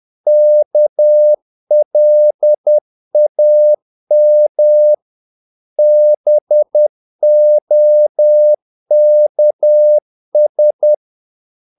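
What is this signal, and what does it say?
Morse code, heard as a keyed tone, about 10 wpm, 600 Hz. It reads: KLAM BOKS
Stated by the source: Morse practice of short Danish words at 10 wpm, from CW med Gnister.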